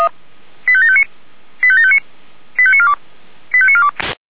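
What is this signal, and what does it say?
Five-tone selective call (ZVEI), heard as quick runs of five tones, tones of about 70 ms. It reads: 85759 85759 85942 85942
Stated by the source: ZVEI five-tone selective call 85729, received by SDR.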